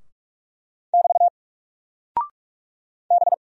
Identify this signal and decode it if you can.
Morse code, heard as a keyed tone, about 45 wpm, 700 Hz. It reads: <BT> B
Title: Single Character (Letter:Number:Punctuation:Prosign) Alt 45wpm